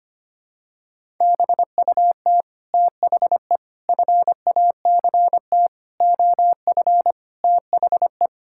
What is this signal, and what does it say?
Morse code, heard as a keyed tone, about 25 wpm, 700 Hz. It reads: BUT THE FACT OF THE
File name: practice / 25wpm / Sentences with 9-Word Phrases - 25wpm